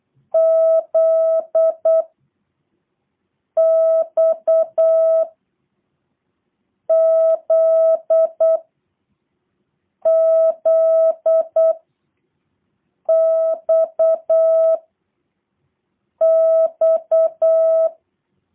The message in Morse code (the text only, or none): Z X Z Z X X